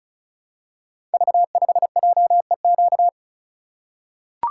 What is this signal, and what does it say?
Morse code, heard as a keyed tone, about 36 wpm, 700 Hz. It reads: V5JEQ
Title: Code Groups - 5 Letter:Numbers - 35wpm